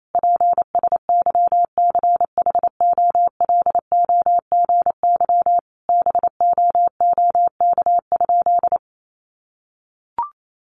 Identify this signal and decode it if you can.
Morse code, heard as a keyed tone, about 28 wpm, 700 Hz. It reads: PSYCHOLOGY BOOK?
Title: Sentences from Top 1400 Words 28wpm